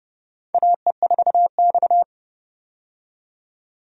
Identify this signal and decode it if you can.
Morse code, heard as a keyed tone, about 30 wpm, 700 Hz. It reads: AE4X
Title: USA Call Signs - 4 Characters - 30wpm